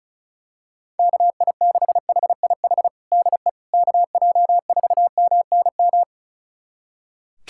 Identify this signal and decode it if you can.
Morse code, heard as a keyed tone, about 36 wpm, 700 Hz. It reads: KI6HIH DE KJ4MNM
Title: USA Callsign DE Callsign 35wpm